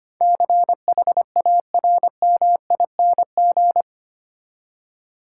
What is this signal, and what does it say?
Morse code, heard as a keyed tone, about 25 wpm, 700 Hz. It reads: CHARMING
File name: Eight Letter Words - 25wpm